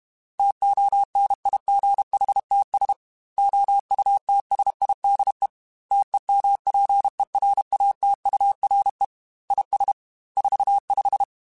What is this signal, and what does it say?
Morse code, heard as a keyed tone, about 32 wpm, 800 Hz. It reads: TONIGHTS OUTSIDE TEMPERATURE IS 45
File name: Weather Sets / Weather Sets-mp3s / Weather 3 17 @30